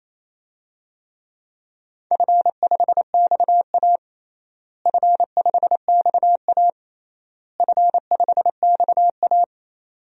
Morse code, heard as a keyed tone, about 28 wpm, 700 Hz. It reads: F5XA F5XA F5XA
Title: DXCC-100 Call Signs - 4 Characters - Repeated 3x - 28wpm